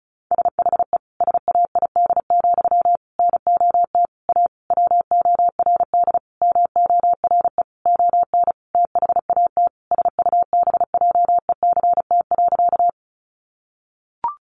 Morse code, heard as a keyed tone, about 34 wpm, 700 Hz. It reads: SHE SAID, NOT A WORD MORE ON THAT SUBJECT.
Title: Sentences from Top 800 Words 35wpm